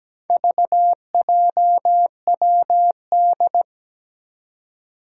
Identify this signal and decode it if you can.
Morse code, heard as a keyed tone, about 17 wpm, 700 Hz. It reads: VJWD
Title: Four Challenging Letter-Number Combinations 17wpm